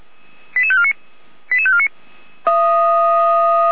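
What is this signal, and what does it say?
Five-tone selective call (ZVEI), heard as quick runs of five tones, tones of about 70 ms. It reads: 88549 88549
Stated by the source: Five-tone ZVEI selective call received by SDR.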